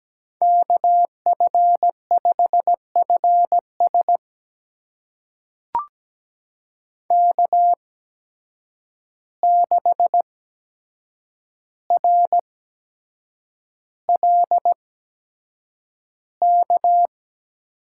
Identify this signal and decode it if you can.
Morse code, heard as a keyed tone, about 17 wpm, spaced as for 7 wpm, 700 Hz. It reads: KF5FS K6RLK
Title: USA Call Signs - 5 Characters - 8x Character Spacing - 17wpm